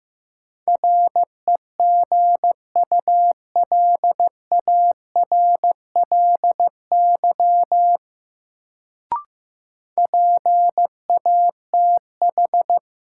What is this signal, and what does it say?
Morse code, heard as a keyed tone, about 15 wpm, 700 Hz. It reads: REGULARLY PATH
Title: Sentences from Top 1400 Words - New Word Drill - 15wpm